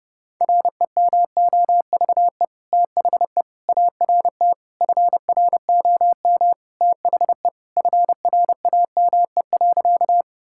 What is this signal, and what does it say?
Morse code, heard as a keyed tone, about 29 wpm, 700 Hz. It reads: REMOVE THE ART FROM THE FRAME.
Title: Sentences from Top 1000 Words - Review Part 6 - 30wpm